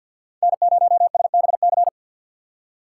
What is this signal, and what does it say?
Morse code, heard as a keyed tone, about 49 wpm, 700 Hz. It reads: N0SBC
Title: USA Call Signs - 5 Characters - 4x Character Spacing - 50wpm